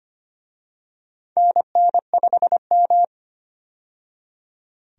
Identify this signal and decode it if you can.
Morse code, heard as a keyed tone, about 25 wpm, 700 Hz. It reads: NN5M
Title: USA Call Signs - 4 Characters - 25wpm